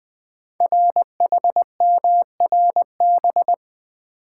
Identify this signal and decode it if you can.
Morse code, heard as a keyed tone, about 20 wpm, 700 Hz. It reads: RHMRB
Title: Code Groups - 5 Letters - 20wpm